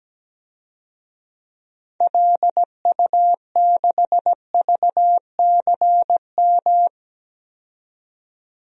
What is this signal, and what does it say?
Morse code, heard as a keyed tone, about 17 wpm, 700 Hz. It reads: LU6VCM
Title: DXCC-100 Call Signs - 6 Characters - 17wpm